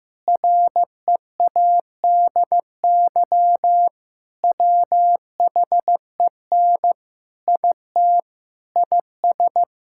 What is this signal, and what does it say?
Morse code, heard as a keyed tone, about 15 wpm, 700 Hz. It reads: READY WHEN IT IS
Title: Sentences from Top 1600 Words 15wpm